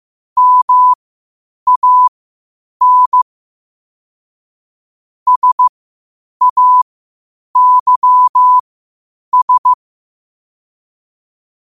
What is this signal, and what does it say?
Morse code, heard as a keyed tone, about 15 wpm, spaced as for 8 wpm, 1000 Hz. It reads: MAN SAYS
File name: qotdfiles / 8wpm